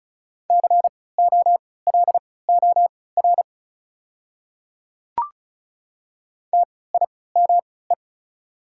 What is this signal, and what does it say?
Morse code, heard as a keyed tone, about 36 wpm, spaced as for 20 wpm, 690 Hz. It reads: COLOR TIME
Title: 3x Character Spacing - Top 500 Words - 35wpm